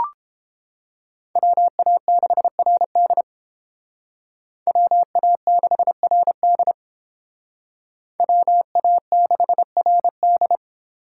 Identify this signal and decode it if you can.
Morse code, heard as a keyed tone, about 29 wpm, 700 Hz. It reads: WA6RD WA6RD WA6RD